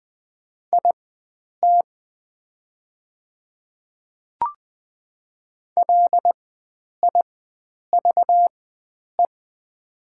Morse code, heard as a keyed tone, about 20 wpm, spaced as for 9 wpm, 700 Hz. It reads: IT LIVE
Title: 4x Character Spacing - Top 300 Words - 20wpm